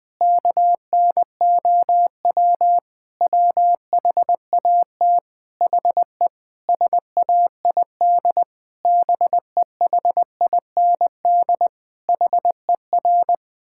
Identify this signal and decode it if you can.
Morse code, heard as a keyed tone, about 21 wpm, 700 Hz. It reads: KNOW WHAT HE SAID BEHIND HER
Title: Sentences from Top 1000 Words - Review Part 6 - 20wpm